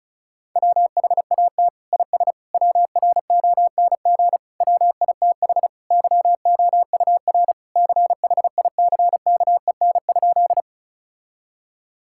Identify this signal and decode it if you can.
Morse code, heard as a keyed tone, about 36 wpm, 700 Hz. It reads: WHAT IS WRONG WITH YOUR CHICKEN?
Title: Sentences from Top 1000 Words - Review Part 5 - 35wpm